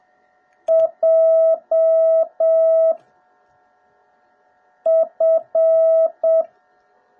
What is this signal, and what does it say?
Morse code, heard as a keyed tone, about 7 wpm, 650 Hz. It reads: J F